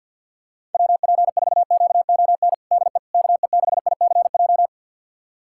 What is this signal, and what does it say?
Morse code, heard as a keyed tone, about 49 wpm, 700 Hz. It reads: WP4QON DE KE6ICJ
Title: USA Callsign DE Callsign 50wpm